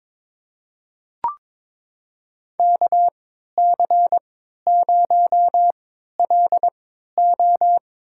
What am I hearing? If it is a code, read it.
Morse code, heard as a keyed tone, about 22 wpm, spaced as for 12 wpm, 700 Hz. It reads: KC0LO
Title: USA Call Signs - 3x Character Spacing - 22wpm